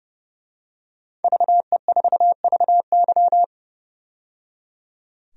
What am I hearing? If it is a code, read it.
Morse code, heard as a keyed tone, about 30 wpm, 700 Hz. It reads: VE4VY
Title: DXCC-100 Call Signs - 5 Characters - 30wpm